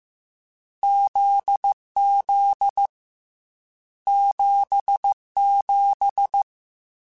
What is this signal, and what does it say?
Morse code, heard as a keyed tone, about 15 wpm, 800 Hz. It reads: ZZ 77
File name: Difficult Combos / Difficult Combos-mp3s / GZ GZ7 @13